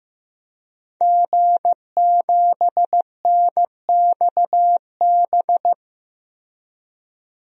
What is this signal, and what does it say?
Morse code, heard as a keyed tone, about 16 wpm, 700 Hz. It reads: G7NXB
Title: DXCC-100 Call Signs - 5 Characters - Repeated 2x - 15wpm